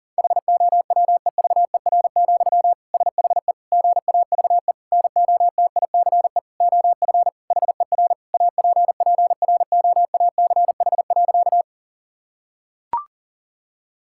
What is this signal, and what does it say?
Morse code, heard as a keyed tone, about 40 wpm, 700 Hz. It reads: HOWEVER, SHE GAVE NOTICE OF HER APPROACH.
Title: Sentences from Top 1000 Words - Review Part 6 - 40wpm